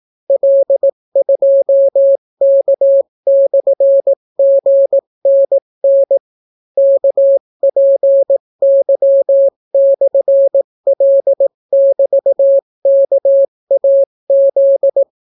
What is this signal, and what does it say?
Morse code, heard as a keyed tone, about 18 wpm, 550 Hz. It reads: L2K/GNN KPY/L<BT>KAZ